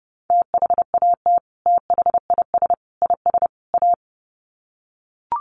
Morse code, heard as a keyed tone, about 30 wpm, 700 Hz. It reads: THAT THIS IS A
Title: Sets of 4 Words 30wpm